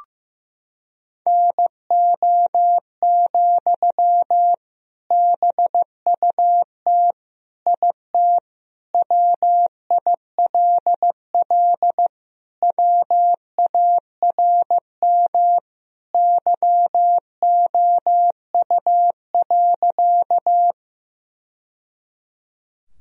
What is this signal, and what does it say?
Morse code, heard as a keyed tone, about 15 wpm, 700 Hz. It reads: NO, BUT IT WILL WARM YOU.
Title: Sentences from Top 1000 Words - Review Part 2 - 15wpm